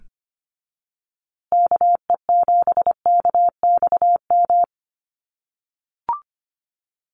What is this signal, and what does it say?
Morse code, heard as a keyed tone, about 25 wpm, 700 Hz. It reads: KE7KXM
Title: USA Call Signs Repeated 2x 25wpm